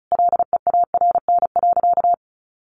Morse code, heard as a keyed tone, about 34 wpm, 700 Hz. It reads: LEARN.
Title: Sentences from Top 1400 Words 35wpm